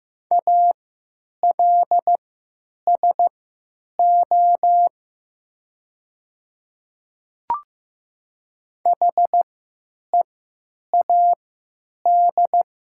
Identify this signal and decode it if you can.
Morse code, heard as a keyed tone, about 15 wpm, spaced as for 9 wpm, 700 Hz. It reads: ALSO HEAD
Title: 3x Character Spacing - Top 300 Words - 15wpm